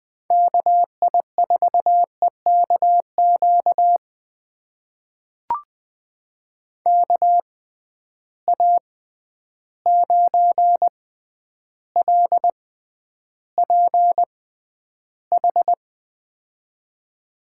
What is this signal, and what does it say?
Morse code, heard as a keyed tone, about 20 wpm, spaced as for 10 wpm, 700 Hz. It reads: KI4EKQ KA9LPH
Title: USA Call Signs - 6 Characters - 6x Character Spacing - 20wpm